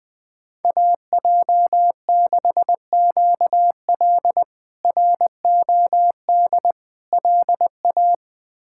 Morse code, heard as a keyed tone, about 20 wpm, 700 Hz. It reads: AJ6QL ROD LA
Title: K1USN SST QSO Practice - 20wpm